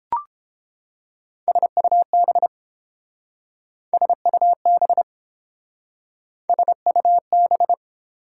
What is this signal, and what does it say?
Morse code, heard as a keyed tone, about 30 wpm, 700 Hz. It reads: SUB SUB SUB